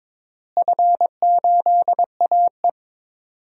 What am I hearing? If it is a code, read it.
Morse code, heard as a keyed tone, about 22 wpm, 700 Hz. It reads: F8AE
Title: DXCC-100 Call Signs - 4 Characters - Repeated 3x - 22wpm